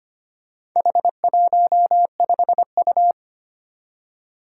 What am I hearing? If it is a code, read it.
Morse code, heard as a keyed tone, about 25 wpm, 700 Hz. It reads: H15U
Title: Four Challenging Letter-Number Combinations 25wpm